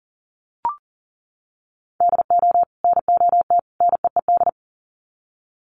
Morse code, heard as a keyed tone, about 40 wpm, 700 Hz. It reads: DO NOT NEED
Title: Sets of 3 Words 40wpm